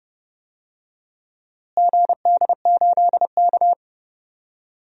Morse code, heard as a keyed tone, about 30 wpm, 700 Hz. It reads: GD8K
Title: DXCC-100 Call Signs - 4 Characters - Repeated 3x - 30wpm